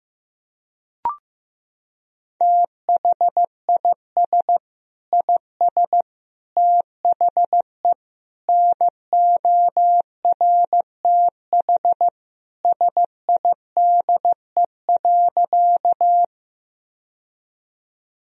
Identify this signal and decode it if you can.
Morse code, heard as a keyed tone, about 15 wpm, 700 Hz. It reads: THIS IS THE NORTH SIDE.
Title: Sentences from Top 1000 Words - Review Part 1 - 15wpm